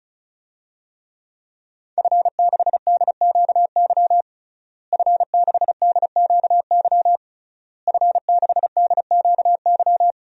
Morse code, heard as a keyed tone, about 36 wpm, 700 Hz. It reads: F6DQY F6DQY F6DQY
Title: DXCC-100 Call Signs - 5 Characters - Repeated 3x - 35wpm